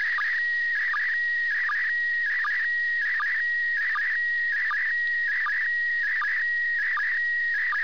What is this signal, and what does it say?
SSTV (slow-scan television), PD 180.
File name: iss sstv